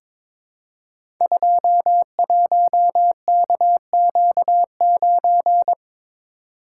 Morse code, heard as a keyed tone, about 22 wpm, 700 Hz. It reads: 21KQ9
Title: Code Groups - 5 Letter:Numbers - 22wpm